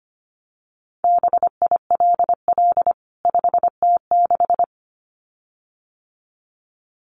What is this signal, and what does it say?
Morse code, heard as a keyed tone, about 25 wpm, 700 Hz. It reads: BILL 5T6
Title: CWT Exchange 25wpm